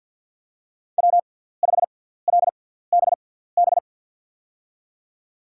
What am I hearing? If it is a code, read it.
Morse code, heard as a keyed tone, about 52 wpm, spaced as for 17 wpm, 690 Hz. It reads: W5LBB
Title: USA Call Signs - 5 Characters - 6x Character Spacing - 50wpm